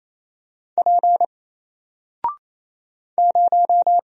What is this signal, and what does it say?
Morse code, heard as a keyed tone, about 28 wpm, 700 Hz. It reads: P 0
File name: Single-Letter-Number-28wpm